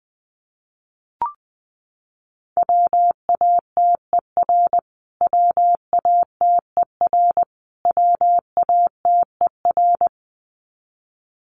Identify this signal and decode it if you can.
Morse code, heard as a keyed tone, about 20 wpm, 700 Hz. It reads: WATER WATER WATER
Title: Top 200 Words Repeated 20wpm